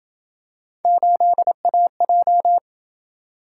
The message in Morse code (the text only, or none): 8AJ